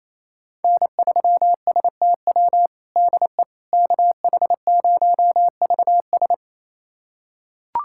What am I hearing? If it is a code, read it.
Morse code, heard as a keyed tone, about 28 wpm, 700 Hz. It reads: N3STW DE KH0VS
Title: USA Callsign DE Callsign - 5 Characters - 28wpm